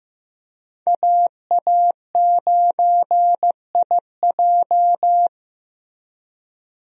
Morse code, heard as a keyed tone, about 15 wpm, 700 Hz. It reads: AA9IJ
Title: USA Call Signs - 5 Characters - Repeated 2x - 15wpm